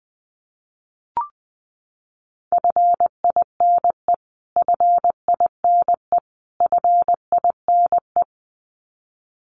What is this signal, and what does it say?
Morse code, heard as a keyed tone, about 20 wpm, 700 Hz. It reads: FINE FINE FINE
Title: Top 500 Words Repeated 20wpm